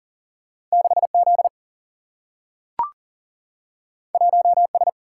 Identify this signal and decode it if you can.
Morse code, heard as a keyed tone, about 40 wpm, 700 Hz. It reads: 6 Z 1 S
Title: Single-Letter Single-Number 40wpm